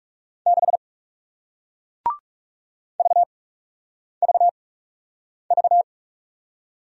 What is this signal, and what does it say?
Morse code, heard as a keyed tone, about 41 wpm, 700 Hz. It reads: 6 V V V